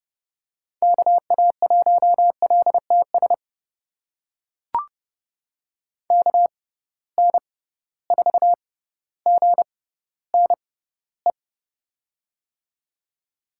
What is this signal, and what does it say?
Morse code, heard as a keyed tone, about 30 wpm, spaced as for 15 wpm, 700 Hz. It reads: KA1LTS KN4GNE
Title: USA Call Signs - 6x Character Spacing - 30wpm